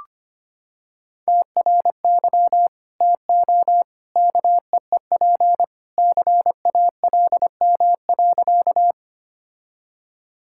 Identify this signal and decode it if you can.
Morse code, heard as a keyed tone, about 25 wpm, 700 Hz. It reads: TRY TO KEEP CALM.
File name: practice / 25wpm / Sentences with 2-Word Phrases - 25wpm